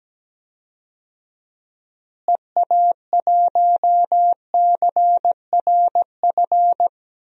Morse code, heard as a keyed tone, about 17 wpm, 700 Hz. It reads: EA1CRF
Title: DXCC-100 Call Signs - 6 Characters - Repeated 2x - 17wpm